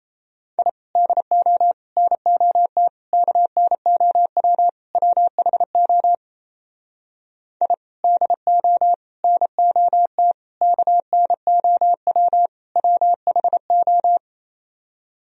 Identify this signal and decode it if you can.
Morse code, heard as a keyed tone, about 37 wpm, 700 Hz. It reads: I DO NOT KNOW WHO I DO NOT KNOW WHO